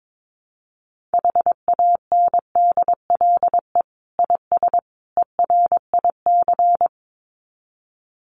Morse code, heard as a keyed tone, about 22 wpm, 700 Hz. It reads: HANDLE IS ERIC
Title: Common QSO Phrases 22wpm